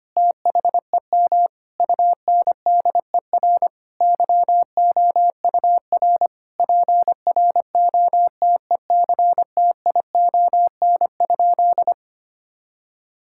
Morse code, heard as a keyed tone, about 25 wpm, 700 Hz. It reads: THEM UNDER YOUR PROTECTION?
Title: Sentences from Top 1600 Words 25wpm